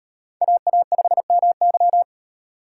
Morse code, heard as a keyed tone, about 34 wpm, 700 Hz. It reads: AA5MY